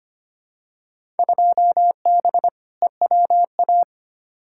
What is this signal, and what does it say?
Morse code, heard as a keyed tone, about 25 wpm, 700 Hz. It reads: 2B EWA
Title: Field Day Exchanges Encore - 25wpm